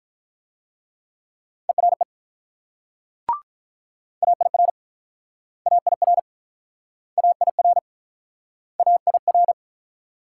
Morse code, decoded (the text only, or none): ERE AIR AIR AIR AIR